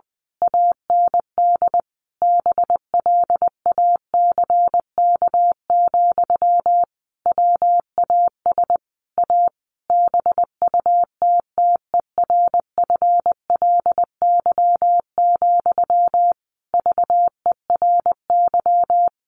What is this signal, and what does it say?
Morse code, heard as a keyed tone, about 20 wpm, 700 Hz. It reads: AND BLACK, WAS A BUTTERFLY, VERY
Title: A Sound Of Thunder 20wpm